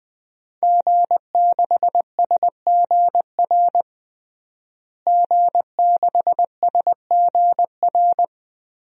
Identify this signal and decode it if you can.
Morse code, heard as a keyed tone, about 20 wpm, 700 Hz. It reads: G6SGR G6SGR
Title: DXCC-100 Call Signs - 5 Characters - Repeated 3x - 20wpm